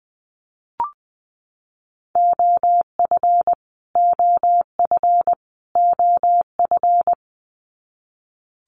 Morse code, heard as a keyed tone, about 20 wpm, 700 Hz. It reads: OF OF OF
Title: Top 300 Words Repeated 20wpm